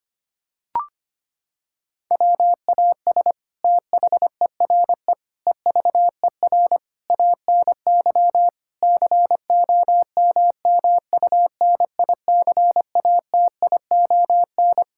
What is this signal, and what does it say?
Morse code, heard as a keyed tone, about 25 wpm, 700 Hz. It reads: WAS THERE EVER ANY COMMUNICATION
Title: Sentences from Top 1000 Words - Review Part 6 - 25wpm